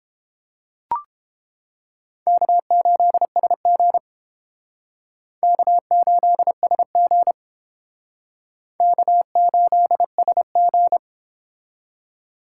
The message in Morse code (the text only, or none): K8SG K8SG K8SG